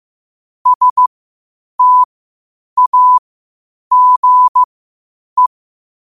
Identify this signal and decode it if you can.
Morse code, heard as a keyed tone, about 15 wpm, spaced as for 8 wpm, 1000 Hz. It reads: STAGE
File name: 8wpm